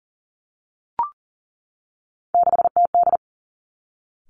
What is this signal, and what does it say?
Morse code, heard as a keyed tone, about 42 wpm, 700 Hz. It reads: 6TD
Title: Three Challenging Letter-Number Combinations 40wpm